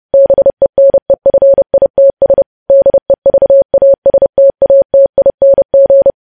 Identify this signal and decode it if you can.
Morse code, heard as a keyed tone, about 30 wpm, 550 Hz. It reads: BENEFITS DEVASTATING
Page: news headlines in morse code at 30 wpm